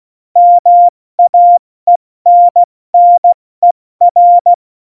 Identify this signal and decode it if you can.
Morse code, heard as a keyed tone, about 16 wpm, 700 Hz. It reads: MAENNER
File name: Morsekurs / Witze / Witz 26. Januar 2021, 70 F80